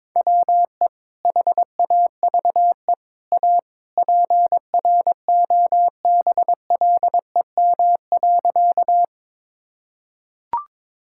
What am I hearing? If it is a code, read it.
Morse code, heard as a keyed tone, about 22 wpm, 700 Hz. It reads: WE HAVE A PROBLEM.
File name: practice / 22wpm / Sentences with 3-Word Phrases - 22wpm